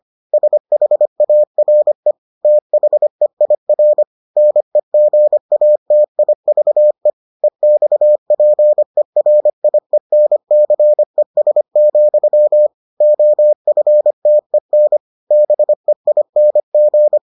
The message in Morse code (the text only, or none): SHARE THEIR NEGATIVE EXPERIENCES, OFTEN BEING